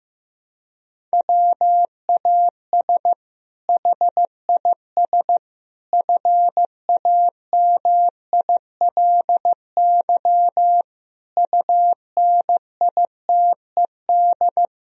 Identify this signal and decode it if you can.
Morse code, heard as a keyed tone, about 15 wpm, 700 Hz. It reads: WAS HIS FAMILY UNITED
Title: Sentences from Top 1000 Words - Review Part 4 - 15wpm